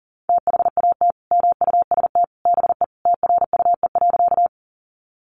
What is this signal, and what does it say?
Morse code, heard as a keyed tone, about 40 wpm, 700 Hz. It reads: THAT MUST BE TRUE.
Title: Sentences from Top 400 Words 40wpm